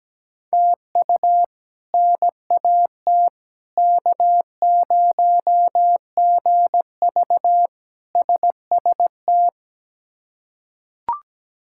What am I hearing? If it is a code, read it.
Morse code, heard as a keyed tone, about 17 wpm, 700 Hz. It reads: TU NAT K0GV SST
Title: K1USN SST QSO Practice - 17wpm